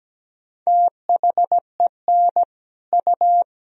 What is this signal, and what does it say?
Morse code, heard as a keyed tone, about 17 wpm, 700 Hz. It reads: THEN U
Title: Sentences from Top 100 Words and Abbreviations 17wpm